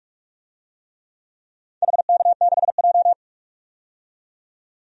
Morse code, heard as a keyed tone, about 46 wpm, 700 Hz. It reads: HK6J
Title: DXCC-100 Call Signs - 4 Characters - 45wpm